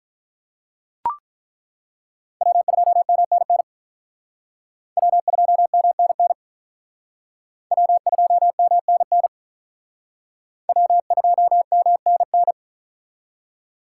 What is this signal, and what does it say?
Morse code, heard as a keyed tone, about 43 wpm, 700 Hz. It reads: W2MNN W2MNN W2MNN W2MNN
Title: Speed Racing - USA Call Signs 35wpm